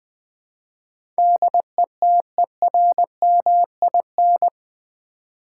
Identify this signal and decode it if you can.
Morse code, heard as a keyed tone, about 20 wpm, 700 Hz. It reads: DETERMIN
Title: Build A Word - 4 Letters At A Time - 10-Letter Words - 20wpm